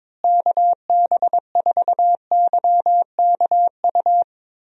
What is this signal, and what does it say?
Morse code, heard as a keyed tone, about 22 wpm, 700 Hz. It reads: KB4YKU
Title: USA Call Signs - 6 Characters - Repeated 2x - 22wpm